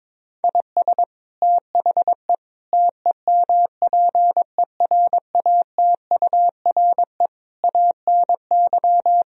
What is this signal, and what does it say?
Morse code, heard as a keyed tone, about 22 wpm, 700 Hz. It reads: IS THE TEMPERATURE ANY
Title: Sentences from Top 1000 Words - Review Part 4 - 22wpm